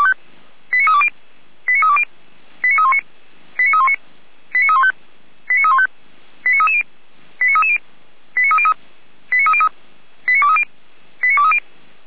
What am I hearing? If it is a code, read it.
Five-tone selective call (ZVEI), heard as quick runs of five tones, tones of about 70 ms. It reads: 89320 89320 89319 89319 89315 89315 89330 89330 89303 89303 89230 89230
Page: /455/SDR/ZVEI/